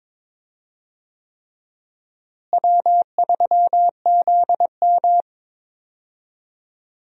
Morse code, heard as a keyed tone, about 22 wpm, 700 Hz. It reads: W3ZM
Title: USA Call Signs - 4 Characters - Repeated 3x - 22wpm